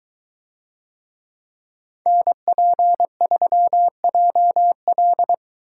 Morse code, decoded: NP3JL

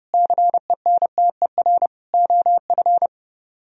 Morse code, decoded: CENTER OF

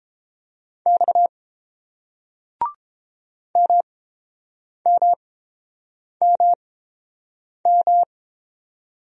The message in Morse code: X MMMM

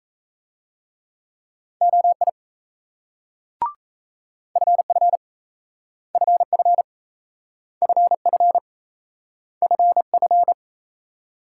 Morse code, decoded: OI FF FF FF FF

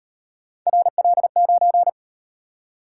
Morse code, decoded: RL9

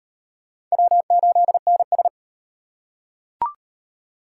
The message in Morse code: W8NS

